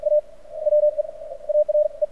Morse code, heard as a keyed tone, about 53 wpm, spaced as for 44 wpm, 600 Hz. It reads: A WE ARE
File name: archived / CRSnet